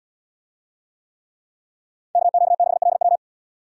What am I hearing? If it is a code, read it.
Morse code, heard as a keyed tone, about 78 wpm, 700 Hz. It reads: KQ6LW